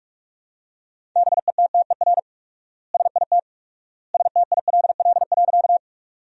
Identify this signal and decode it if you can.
Morse code, heard as a keyed tone, about 41 wpm, 700 Hz. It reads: BETTER SIT STILL.
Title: 4x Word-Spacing - Sentences from Top 500 Words - 45wpm